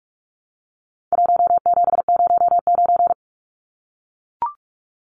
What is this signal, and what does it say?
Morse code, heard as a keyed tone, about 44 wpm, 700 Hz. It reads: 1709
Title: Numbers - Contest Style - 4-Digit 45wpm